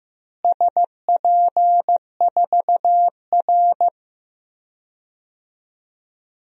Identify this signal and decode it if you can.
Morse code, heard as a keyed tone, about 15 wpm, 700 Hz. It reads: SP4R
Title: DXCC-100 Call Signs - 4 Characters - Repeated 2x - 15wpm